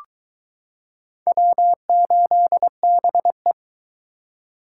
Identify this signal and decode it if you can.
Morse code, heard as a keyed tone, about 22 wpm, 700 Hz. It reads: W8BE